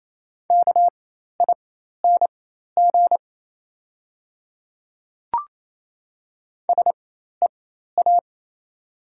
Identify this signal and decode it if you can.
Morse code, heard as a keyed tone, about 28 wpm, spaced as for 13 wpm, 700 Hz. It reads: KING SEA
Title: 4x Character Spacing - Top 500 Words - 28wpm